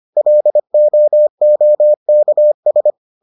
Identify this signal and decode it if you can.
Morse code, heard as a keyed tone, about 25 wpm, 600 Hz. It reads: LOOKS